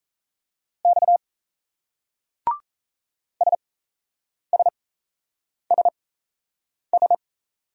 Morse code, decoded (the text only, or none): X SSSS